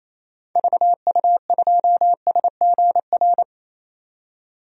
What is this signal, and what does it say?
Morse code, heard as a keyed tone, about 28 wpm, 700 Hz. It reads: VU2SGR
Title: DXCC-100 Call Signs - 6 Characters - 28wpm